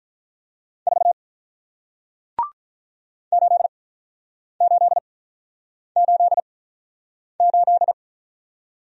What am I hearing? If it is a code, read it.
Morse code, heard as a keyed tone, about 48 wpm, spaced as for 9 wpm, 700 Hz. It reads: 4 8888